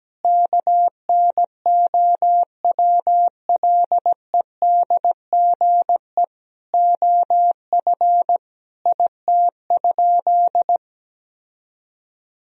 Morse code, heard as a keyed tone, about 17 wpm, 700 Hz. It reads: KNOWLEDGE OF IT?
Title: Sentences from Top 1000 Words - Review Part 3 - 17wpm